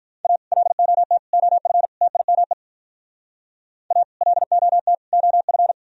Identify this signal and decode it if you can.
Morse code, heard as a keyed tone, about 52 wpm, 700 Hz. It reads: A LOT OF TIME A LOT OF